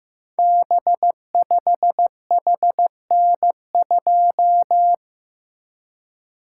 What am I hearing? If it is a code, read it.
Morse code, heard as a keyed tone, about 15 wpm, 700 Hz. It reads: B5HN2
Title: Code Groups - 5 Letter:Numbers - 15wpm